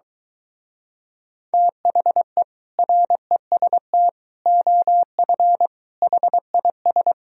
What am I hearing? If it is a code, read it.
Morse code, heard as a keyed tone, about 22 wpm, 700 Hz. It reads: THE REST OF HIS